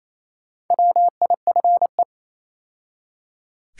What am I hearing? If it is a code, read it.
Morse code, heard as a keyed tone, about 27 wpm, 700 Hz. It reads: WIFE
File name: Easy Words - Speed-Racing with Hints - 22wpm